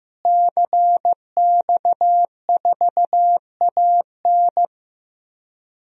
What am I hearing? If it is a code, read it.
Morse code, heard as a keyed tone, about 15 wpm, 700 Hz. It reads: CX4AN
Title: DXCC-100 Call Signs - 5 Characters - Repeated 2x - 15wpm